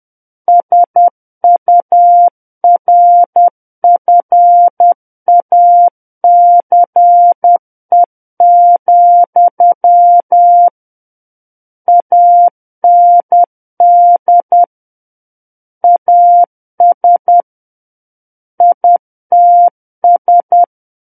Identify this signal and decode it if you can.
Morse code, heard as a keyed tone, about 10 wpm, 700 Hz. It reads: SURFACE, AND AS ITS
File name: Book One / 01-Chapter 1 - 10 WPM